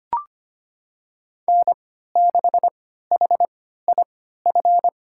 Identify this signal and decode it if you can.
Morse code, heard as a keyed tone, about 25 wpm, spaced as for 14 wpm, 700 Hz. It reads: N6HIF
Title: USA Call Signs - 3x Character Spacing - 25wpm